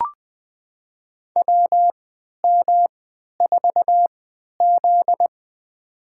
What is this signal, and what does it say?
Morse code, heard as a keyed tone, about 21 wpm, spaced as for 11 wpm, 700 Hz. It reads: WM4Z
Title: USA Call Signs - 4 Characters - 3x Character Spacing - 20wpm